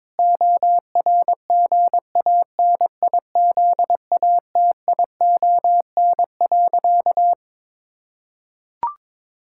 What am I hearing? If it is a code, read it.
Morse code, heard as a keyed tone, about 21 wpm, 700 Hz. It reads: ORGANIZATION.
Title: Sentences from Top 1000 Words - Review Part 6 - 22wpm